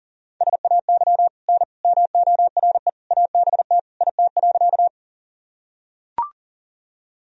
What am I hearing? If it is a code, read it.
Morse code, heard as a keyed tone, about 37 wpm, 700 Hz. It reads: SAY N MORE ABT IT.
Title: Sentences from Top 200 Words and Abbreviations 40wpm